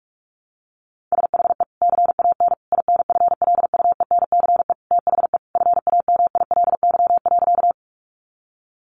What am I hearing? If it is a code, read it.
Morse code, heard as a keyed tone, about 44 wpm, 700 Hz. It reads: SHE CAN INFLUENCE THE FAMILY.
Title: Sentences from Top 1100 Words 45wpm